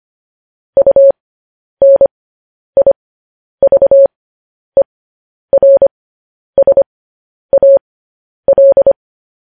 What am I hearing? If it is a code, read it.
Morse code, heard as a keyed tone, about 25 wpm, spaced as for 10 wpm, 550 Hz. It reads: UNIVERSAL
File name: podcast / News headlines-1x-25x10